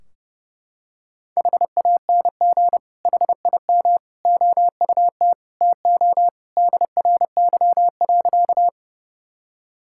Morse code, heard as a keyed tone, about 30 wpm, 700 Hz. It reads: HANG HIM OUT TO DRY.
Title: Sentences from Top 1600 Words 30wpm